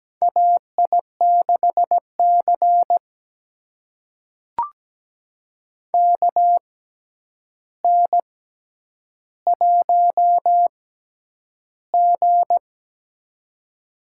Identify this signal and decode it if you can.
Morse code, heard as a keyed tone, about 17 wpm, spaced as for 9 wpm, 700 Hz. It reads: AI6C KN1G
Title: USA Call Signs - 4 Characters - 6x Character Spacing - 17wpm